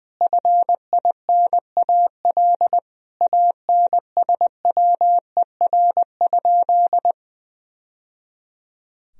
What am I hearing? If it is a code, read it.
Morse code, heard as a keyed tone, about 20 wpm, 700 Hz. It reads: FINAL ANSWER?
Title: Sentences from Top 1000 Words - Review Part 2 - 20wpm